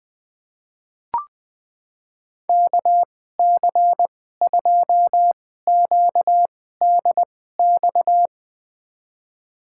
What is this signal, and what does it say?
Morse code, heard as a keyed tone, about 20 wpm, spaced as for 14 wpm, 700 Hz. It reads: KC2QDX
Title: USA Call Signs - 6 Characters - 2x Character Spacing - 20wpm